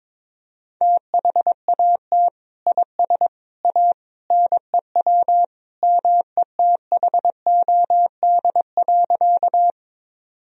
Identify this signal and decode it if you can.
Morse code, heard as a keyed tone, about 22 wpm, 700 Hz. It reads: THAT IS A NEW METHOD.
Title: Sentences from Top 1000 Words - Review Part 3 - 22wpm